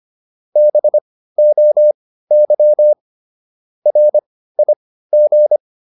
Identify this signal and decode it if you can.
Morse code, heard as a keyed tone, about 25 wpm, spaced as for 15 wpm, 600 Hz. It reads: BOY RIG